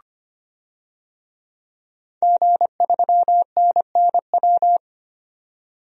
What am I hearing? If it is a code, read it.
Morse code, heard as a keyed tone, about 25 wpm, 700 Hz. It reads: G3NNW
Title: DXCC-100 Call Signs - 5 Characters - 25wpm